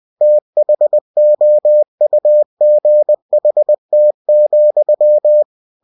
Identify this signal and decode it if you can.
Morse code, heard as a keyed tone, about 20 wpm, 600 Hz. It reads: THOUGHT,
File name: cw-books / winnie-the-pooh / winnie-20wpm-page-0076